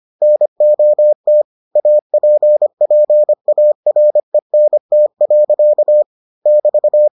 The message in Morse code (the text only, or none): NOT APPARENT. <BT>